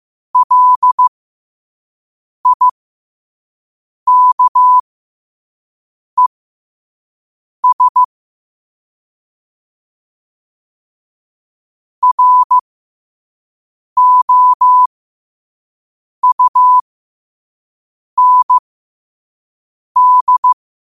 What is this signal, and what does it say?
Morse code, heard as a keyed tone, about 15 wpm, spaced as for 5 wpm, 1000 Hz. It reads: LIKES ROUND
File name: qotdfiles / 5wpm